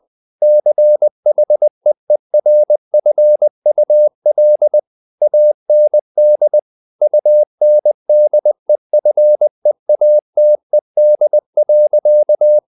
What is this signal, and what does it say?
Morse code, heard as a keyed tone, about 20 wpm, 600 Hz. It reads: CHEERFUL AND UNDEFEATED.
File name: the old man and the sea / the old man and the sea part 1 20wpm